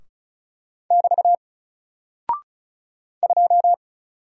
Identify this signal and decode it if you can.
Morse code, heard as a keyed tone, about 35 wpm, 700 Hz. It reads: <BT> 2
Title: Single Character (Letter:Number:Punctuation:Prosign) 35wpm